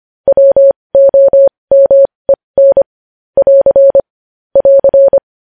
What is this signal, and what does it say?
Morse code, heard as a keyed tone, about 25 wpm, spaced as for 20 wpm, 550 Hz. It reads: WOMEN <AR> <AR>
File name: podcast / News headlines-1x-25x20